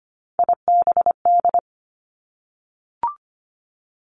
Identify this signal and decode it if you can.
Morse code, heard as a keyed tone, about 25 wpm, 700 Hz. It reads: IBD
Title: Three Challenging Letter-Number Combinations 25wpm